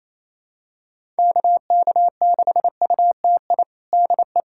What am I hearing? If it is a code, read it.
Morse code, heard as a keyed tone, about 28 wpm, 700 Hz. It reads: KK6UTI DE
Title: USA Callsign DE Callsign 28wpm